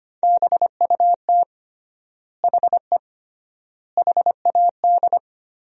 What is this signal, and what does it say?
Morse code, heard as a keyed tone, about 25 wpm, 700 Hz. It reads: BUT HE HAD
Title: 3x Word-Spacing - Sentences from Top 100 Words - 25wpm